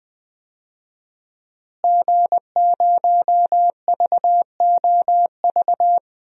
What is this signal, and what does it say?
Morse code, heard as a keyed tone, about 20 wpm, 700 Hz. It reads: G0VOV
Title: DXCC-100 Call Signs - 5 Characters - Repeated 2x - 20wpm